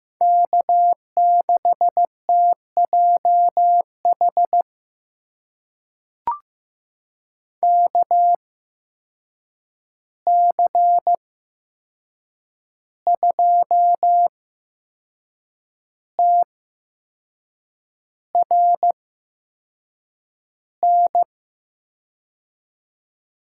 Morse code, heard as a keyed tone, about 15 wpm, spaced as for 6 wpm, 700 Hz. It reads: K6TJH KC2TRN
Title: USA Call Signs - 8x Character Spacing - 15wpm